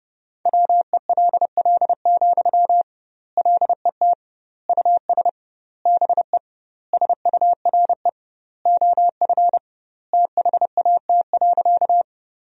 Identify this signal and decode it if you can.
Morse code, heard as a keyed tone, about 30 wpm, 700 Hz. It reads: WELL, LET US BE SURE OF THAT.
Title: 2x Word-Spacing - Sentences from Top 300 Words - 30wpm